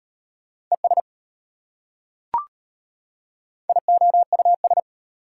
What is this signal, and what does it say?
Morse code, heard as a keyed tone, about 32 wpm, 700 Hz. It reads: ES IOUS